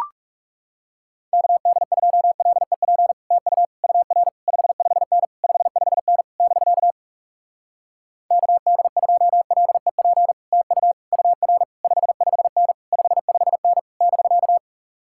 Morse code, heard as a keyed tone, about 40 wpm, 700 Hz. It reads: KD2LEP TU UR 55N 55N <BK> KD2LEP TU UR 55N 55N <BK>